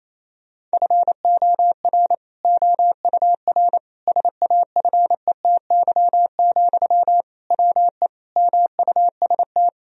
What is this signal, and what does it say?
Morse code, heard as a keyed tone, about 28 wpm, 700 Hz. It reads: FOR OUR SAFETY, WE MUST
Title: Sentences from Top 1000 Words - Review Part 4 - 28wpm